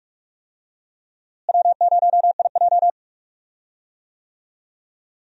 Morse code, heard as a keyed tone, about 45 wpm, 700 Hz. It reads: W0IJ
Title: USA Call Signs - 4 Characters - Repeated 2x - 45wpm